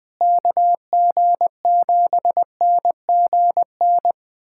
Morse code, heard as a keyed tone, about 21 wpm, 700 Hz. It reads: KG7NGN
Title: USA Callsign DE Callsign - 6 Characters - 20wpm